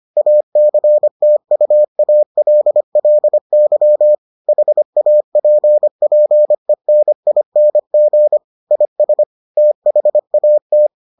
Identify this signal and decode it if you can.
Morse code, heard as a keyed tone, about 25 wpm, 600 Hz. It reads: ACTUALLY HAPPENING IS THAT